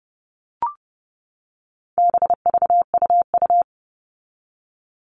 Morse code, heard as a keyed tone, about 30 wpm, 700 Hz. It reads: BVUU
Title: All Cut Numbers - 4-Digit 30wpm